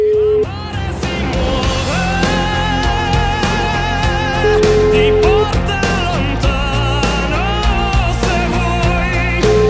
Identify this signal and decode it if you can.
Ringback tone (Europe, 425 Hz), with noise.